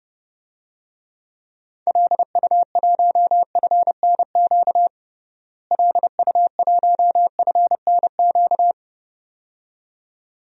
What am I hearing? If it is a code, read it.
Morse code, heard as a keyed tone, about 32 wpm, 700 Hz. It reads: LU1FNQ LU1FNQ
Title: DXCC-100 Call Signs - 6 Characters - Repeated 2x - 30wpm